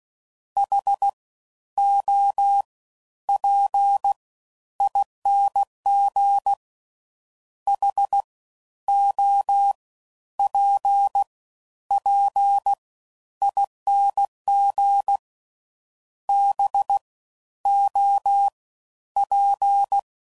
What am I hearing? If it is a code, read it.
Morse code, heard as a keyed tone, about 16 wpm, 800 Hz. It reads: H O P ING H O P P ING B O P